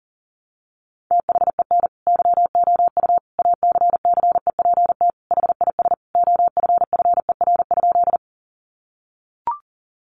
Morse code, heard as a keyed tone, about 40 wpm, 700 Hz. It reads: THEN YOU ACCEPT HIS OFFER?
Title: Sentences from Top 1300 Words 40wpm